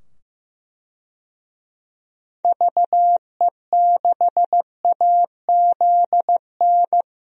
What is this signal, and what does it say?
Morse code, heard as a keyed tone, about 15 wpm, 700 Hz. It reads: VE6AZN
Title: DXCC-100 Call Signs - 6 Characters - 15wpm